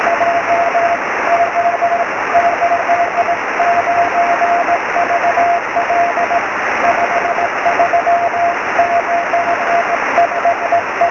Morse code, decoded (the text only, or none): none